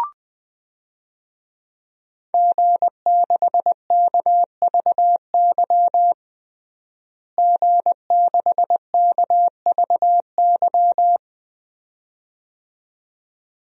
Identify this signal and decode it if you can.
Morse code, heard as a keyed tone, about 20 wpm, 700 Hz. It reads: G6KVY G6KVY
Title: DXCC-100 Call Signs - 5 Characters - Repeated 2x - 20wpm